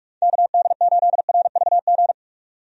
Morse code, heard as a keyed tone, about 44 wpm, 700 Hz. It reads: KD8RVG
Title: USA Call Signs - 6 Characters - 45wpm